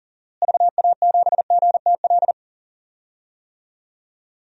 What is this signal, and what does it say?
Morse code, heard as a keyed tone, about 40 wpm, 700 Hz. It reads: VA7GTL
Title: DXCC-100 Call Signs - 6 Characters - Repeated 3x - 40wpm